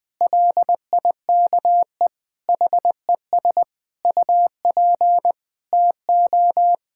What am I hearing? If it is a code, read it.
Morse code, heard as a keyed tone, about 21 wpm, 700 Hz. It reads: LIKE HES UP TO
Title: Sets of 9 Words - 20wpm